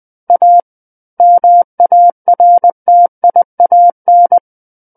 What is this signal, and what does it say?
Morse code, heard as a keyed tone, about 20 wpm, 700 Hz. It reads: A MARTIAN
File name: 20 WPM